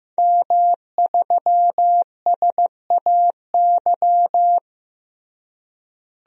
Morse code, heard as a keyed tone, about 15 wpm, 700 Hz. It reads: M3SAY
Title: DXCC-100 Call Signs - 5 Characters - Repeated 3x - 15wpm